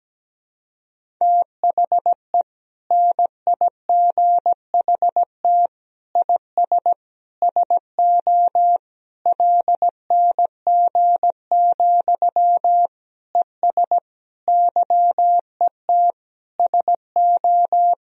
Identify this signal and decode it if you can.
Morse code, heard as a keyed tone, about 17 wpm, 700 Hz. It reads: THE NIGHT IS SO LNG, ES YET SO
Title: Sentences from Top 500 Words and Abbreviations 17wpm